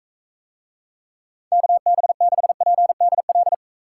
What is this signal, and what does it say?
Morse code, heard as a keyed tone, about 42 wpm, 700 Hz. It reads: KB6PDL